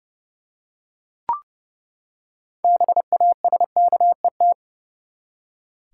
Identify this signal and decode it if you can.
Morse code, heard as a keyed tone, about 32 wpm, 700 Hz. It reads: BASKET
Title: Six Letter Words - 30wpm